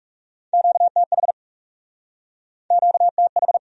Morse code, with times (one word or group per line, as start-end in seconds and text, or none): 0.5-1.3 QTH
2.7-3.6 QTH